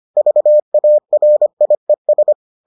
Morse code, heard as a keyed tone, about 25 wpm, 600 Hz. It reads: VARIES